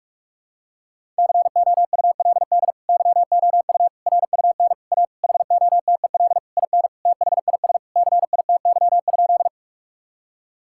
Morse code, heard as a keyed tone, about 41 wpm, 700 Hz. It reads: COULD YOU RUN A HOTEL IN THIS CITY?
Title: Sentences from Top 1600 Words 45wpm